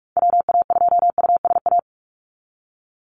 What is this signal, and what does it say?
Morse code, heard as a keyed tone, about 44 wpm, 700 Hz. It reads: WA2USA
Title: NA SOTA Activator Callsigns 45wpm